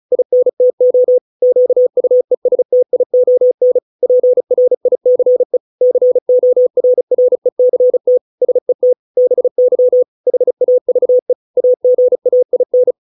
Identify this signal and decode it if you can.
Morse code, heard as a keyed tone, about 34 wpm, 500 Hz. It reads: INTO QUESTION PRICE CORRECT SET BY HAVE AGAIN